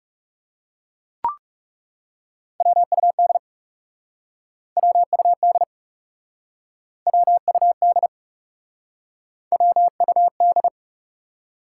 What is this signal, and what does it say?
Morse code, heard as a keyed tone, about 40 wpm, 700 Hz. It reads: WUD WUD WUD WUD